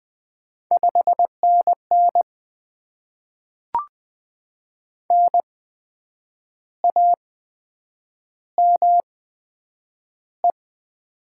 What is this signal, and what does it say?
Morse code, heard as a keyed tone, about 20 wpm, spaced as for 8 wpm, 700 Hz. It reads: 5NN NAME
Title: Top 20 QSO Elements - 8x Character Spacing - 20wpm